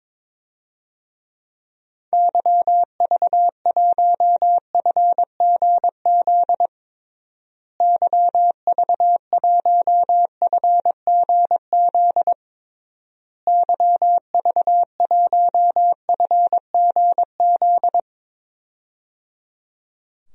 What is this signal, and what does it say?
Morse code, heard as a keyed tone, about 22 wpm, 700 Hz. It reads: YV1FGZ YV1FGZ YV1FGZ
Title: DXCC-100 Call Signs - 6 Characters - Repeated 3x - 22wpm